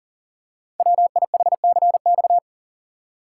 Morse code, heard as a keyed tone, about 38 wpm, 690 Hz. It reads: WIHCX
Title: Code Groups - 5 Letters - 40wpm